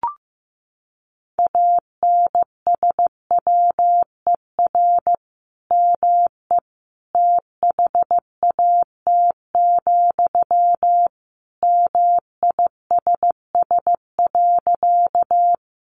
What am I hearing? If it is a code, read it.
Morse code, heard as a keyed tone, about 15 wpm, 700 Hz. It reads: ANSWER ME THAT, MISS.
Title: Sentences from Top 500 Words 15wpm